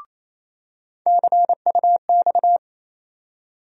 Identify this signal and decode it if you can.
Morse code, heard as a keyed tone, about 28 wpm, 700 Hz. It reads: CUX
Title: USA Call Sign Postfixes - 3 Letters 28wpm